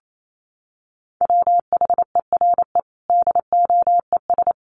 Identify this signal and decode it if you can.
Morse code, heard as a keyed tone, about 28 wpm, 700 Hz. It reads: WHERE DOES
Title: Sentences from Top 1400 Words 28wpm